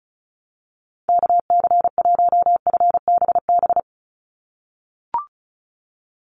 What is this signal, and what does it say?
Morse code, heard as a keyed tone, about 34 wpm, 700 Hz. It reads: KC1FBB
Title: USA Call Signs 35wpm